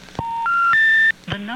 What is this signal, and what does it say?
Special information tone, with speech.